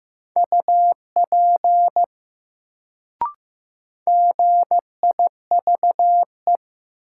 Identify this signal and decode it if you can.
Morse code, heard as a keyed tone, about 15 wpm, 700 Hz. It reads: UP GIVE
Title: Sets of 2 Words - New Word Drill - 15wpm